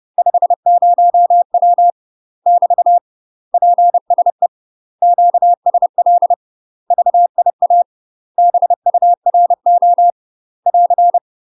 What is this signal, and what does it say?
Morse code, heard as a keyed tone, about 30 wpm, 700 Hz. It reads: 50W <BT> PSE QSL VIA BURO <AR>
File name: UK MASTER QSOs TRAINING / 30WPM / UK MASTER QSOs 0031